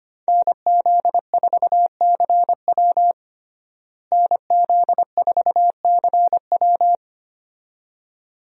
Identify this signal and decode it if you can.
Morse code, heard as a keyed tone, about 25 wpm, 700 Hz. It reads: NZ4CW NZ4CW
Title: USA Call Signs - 5 Characters - Repeated 2x - 25wpm